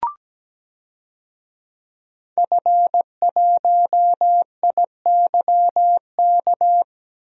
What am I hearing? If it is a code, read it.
Morse code, heard as a keyed tone, about 17 wpm, 700 Hz. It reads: F1IYK
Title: DXCC-100 Call Signs - 5 Characters - 17wpm